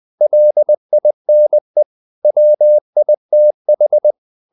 Morse code, heard as a keyed tone, about 20 wpm, 600 Hz. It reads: LINE WITH
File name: the old man and the sea part 2 20wpm